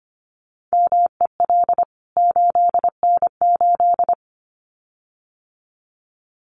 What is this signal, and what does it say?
Morse code, heard as a keyed tone, about 25 wpm, 700 Hz. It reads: MEL 8N8
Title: CWT Exchange 25wpm